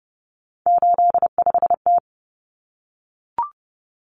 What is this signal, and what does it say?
Morse code, heard as a keyed tone, about 28 wpm, 700 Hz. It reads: 85T
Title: Common Cut Numbers - 3-Digit 30wpm